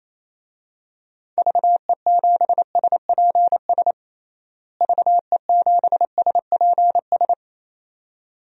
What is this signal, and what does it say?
Morse code, heard as a keyed tone, about 28 wpm, 700 Hz. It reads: VE7SPS VE7SPS
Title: DXCC-100 Call Signs - 6 Characters - Repeated 2x - 28wpm